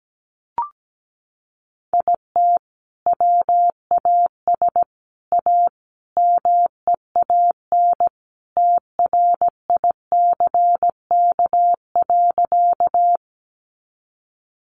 Morse code, heard as a keyed tone, about 17 wpm, 700 Hz. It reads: IT WAS A MEAN TRICK.
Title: Sentences from Top 1500 Words 17wpm